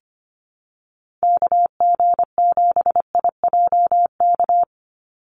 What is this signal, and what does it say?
Morse code, heard as a keyed tone, about 25 wpm, 700 Hz. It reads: KG7IJK
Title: USA Call Signs 25wpm